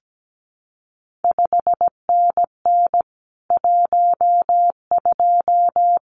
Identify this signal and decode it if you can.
Morse code, heard as a keyed tone, about 17 wpm, 700 Hz. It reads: 5NN 12
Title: CQ WW DX Contest Exchange Encore - 17wpm